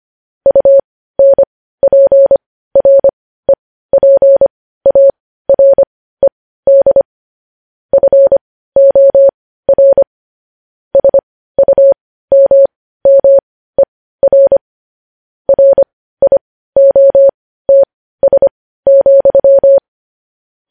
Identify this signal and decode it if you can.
Morse code, heard as a keyed tone, about 25 wpm, spaced as for 15 wpm, 550 Hz. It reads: UNPREPARED FOR SUMMER RIOTS,
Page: news headlines in morse code at 15 wpm